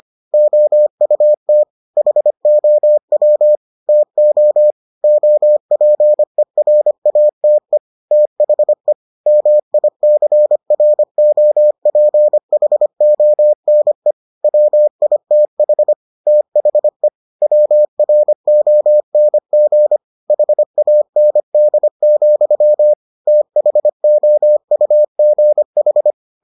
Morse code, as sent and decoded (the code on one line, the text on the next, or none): --- ..- - / .... --- .-- / - --- / --- .--. .-. .- - . / - .... / -- .. -.-. .-. --- .--. .... --- -. / .-- .. - .... / - .... / .-- .-. --- -. --. / .... .- -. -.. --..-- / - .... --- ..- --. ....
OUT HOW TO OPERATE THE MICROPHONE WITH THE WRONG HAND, THOUGH